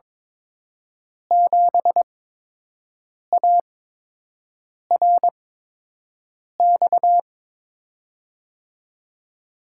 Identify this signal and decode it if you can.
Morse code, heard as a keyed tone, about 22 wpm, spaced as for 6 wpm, 700 Hz. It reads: 7ARX
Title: USA Call Signs - 8x Character Spacing - 22wpm